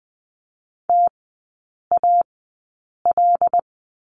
Morse code, heard as a keyed tone, about 20 wpm, spaced as for 8 wpm, 700 Hz. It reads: TAL
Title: 074 Lesson 13e - Letter L - Three Characters 20wpm